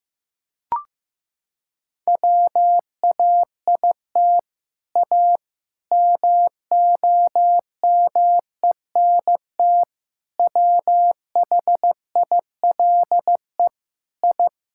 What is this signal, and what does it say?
Morse code, heard as a keyed tone, about 15 wpm, 700 Hz. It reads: WAIT A MOMENT WHILE I
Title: Sentences from Top 1000 Words - Review Part 5 - 15wpm